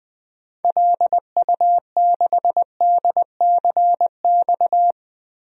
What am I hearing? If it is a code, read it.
Morse code, heard as a keyed tone, about 21 wpm, 700 Hz. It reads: LU6DCX
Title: DXCC-100 Call Signs - 6 Characters - 20wpm